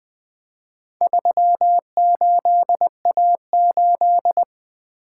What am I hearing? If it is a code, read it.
Morse code, heard as a keyed tone, about 20 wpm, 700 Hz. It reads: 38A8
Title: Four Letter-Number Combinations - 20wpm